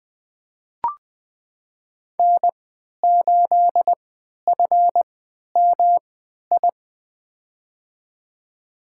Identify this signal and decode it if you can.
Morse code, heard as a keyed tone, about 20 wpm, spaced as for 11 wpm, 700 Hz. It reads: N8FMI
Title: USA Call Signs - 5 Characters - 3x Character Spacing - 20wpm